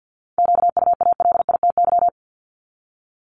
Morse code, heard as a keyed tone, about 50 wpm, 690 Hz. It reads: QUALITY
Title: Multi-letter Phonemes 50wpm